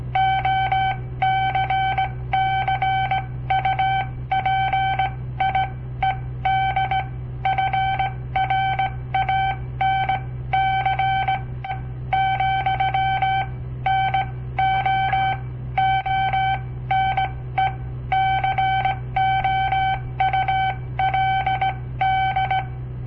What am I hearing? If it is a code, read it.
Morse code, heard as a keyed tone, about 18 wpm, spaced as for 14 wpm, 770 Hz. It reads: OCCUPIED FRANC , NO ONE COULD
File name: papertapes / local / Code Practice-01